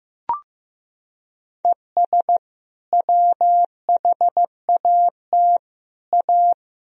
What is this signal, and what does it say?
Morse code, heard as a keyed tone, about 15 wpm, 700 Hz. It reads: ES WHAT A
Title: Sentences from Top 500 Words and Abbreviations 15wpm